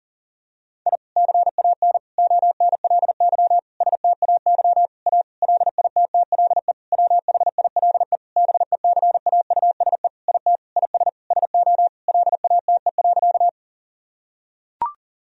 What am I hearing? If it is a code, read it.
Morse code, heard as a keyed tone, about 40 wpm, 700 Hz. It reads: I CAN ONLY STAY A LITTLE WHILE BECAUSE IT IS SO LATE.
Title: Sentences from Top 1000 Words - Review Part 3 - 40wpm